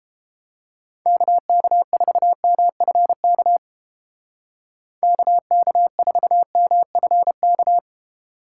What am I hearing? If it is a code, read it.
Morse code, heard as a keyed tone, about 30 wpm, 700 Hz. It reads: KK4MFK KK4MFK